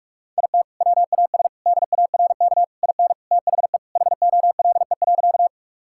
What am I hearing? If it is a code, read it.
Morse code, heard as a keyed tone, about 46 wpm, 700 Hz. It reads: IT WAS DARK IN THE HOLE.
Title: Sentences from Top 1600 Words 45wpm